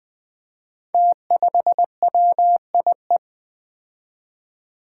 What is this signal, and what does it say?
Morse code, heard as a keyed tone, about 20 wpm, 700 Hz. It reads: T5WIE
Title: Five Challenging Letter-Number Combinations 20wpm